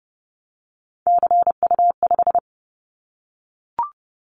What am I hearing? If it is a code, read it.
Morse code, heard as a keyed tone, about 30 wpm, 700 Hz. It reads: CU5
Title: Callsign Prefixes for Top 50 DXCC - 30wpm